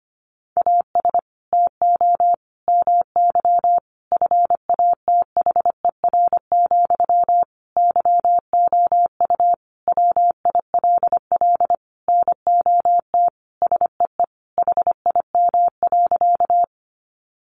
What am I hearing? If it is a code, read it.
Morse code, heard as a keyed tone, about 25 wpm, 700 Hz. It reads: AS TO MY FATHER, YOU WILL NOT SEE HIM.
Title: Sentences from Top 600 Words 25wpm